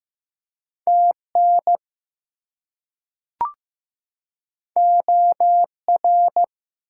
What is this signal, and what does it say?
Morse code, heard as a keyed tone, about 15 wpm, 700 Hz. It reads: TN OR
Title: US States and Canadian Provinces 15wpm